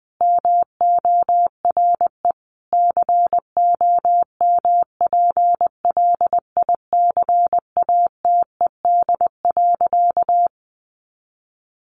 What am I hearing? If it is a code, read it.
Morse code, heard as a keyed tone, about 20 wpm, 700 Hz. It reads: MORE COMPLICATED.
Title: Sentences from Top 1200 Words 20wpm